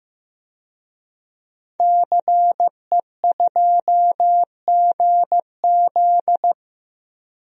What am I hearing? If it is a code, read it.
Morse code, heard as a keyed tone, about 16 wpm, 700 Hz. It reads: CE2GZ
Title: DXCC-100 Call Signs - 5 Characters - 15wpm